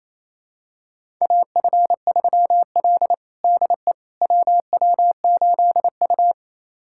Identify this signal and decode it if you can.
Morse code, heard as a keyed tone, about 28 wpm, 700 Hz. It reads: AF3L DE WW8U
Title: USA Callsign DE Callsign - 4 Characters - 28wpm